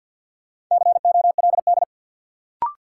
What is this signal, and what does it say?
Morse code, heard as a keyed tone, about 49 wpm, 700 Hz. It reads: COLD
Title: Sentences from Top 400 Words - New Word Drill - 50wpm